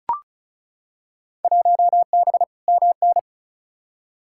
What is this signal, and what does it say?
Morse code, heard as a keyed tone, about 36 wpm, 700 Hz. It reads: 1B MN
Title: Field Day Exchanges Encore - 35wpm